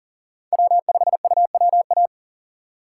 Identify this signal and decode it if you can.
Morse code, heard as a keyed tone, about 37 wpm, 700 Hz. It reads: W5UWA
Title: USA Call Signs - 5 Characters - 3x Character Spacing - 40wpm